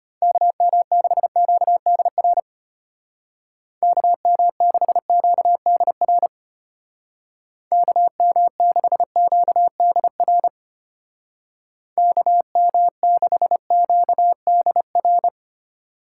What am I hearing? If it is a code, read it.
Morse code, heard as a keyed tone, about 33 wpm, 700 Hz. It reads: KM6QDR KM6QDR KM6QDR KM6QDR